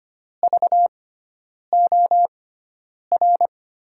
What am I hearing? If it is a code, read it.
Morse code, heard as a keyed tone, about 25 wpm, spaced as for 9 wpm, 700 Hz. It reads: VOR
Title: USA Call Signs - 6 Characters - 6x Character Spacing - 25wpm